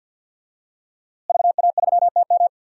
Morse code, heard as a keyed tone, about 52 wpm, 700 Hz. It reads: VA3TM
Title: DXCC-100 Call Signs - 5 Characters - Repeated 2x - 50wpm